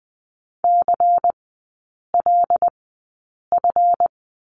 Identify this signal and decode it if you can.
Morse code, heard as a keyed tone, about 20 wpm, spaced as for 8 wpm, 700 Hz. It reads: CLF
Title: 152 Lesson 25e - Letter p - Three Characters 20wpm